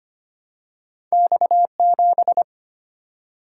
Morse code, heard as a keyed tone, about 25 wpm, 700 Hz. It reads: X7
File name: Two Letter-Number Combinations - 25wpm